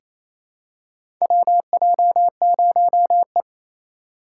Morse code, heard as a keyed tone, about 28 wpm, 700 Hz. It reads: WJ0E
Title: USA Call Signs - 4 Characters - 6x Character Spacing - 28wpm